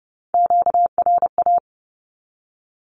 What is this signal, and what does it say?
Morse code, heard as a keyed tone, about 30 wpm, 700 Hz. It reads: QRA
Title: Q-Codes 30wpm